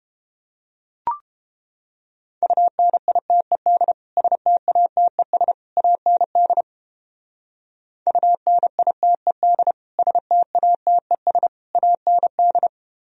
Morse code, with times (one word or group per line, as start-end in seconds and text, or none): 2.4-3.9 UNITED
4.2-5.5 STATES
5.8-6.6 AND
8.1-9.7 UNITED
10.0-11.5 STATES
11.8-12.7 AND